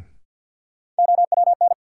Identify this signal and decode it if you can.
Morse code, heard as a keyed tone, about 53 wpm, 700 Hz. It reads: OWN
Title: Sentences from Top 300 Words - New Word Drill - 50wpm